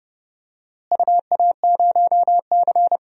Morse code, heard as a keyed tone, about 32 wpm, 700 Hz. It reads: UA0C
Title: DXCC-100 Call Signs - 4 Characters - Repeated 2x - 30wpm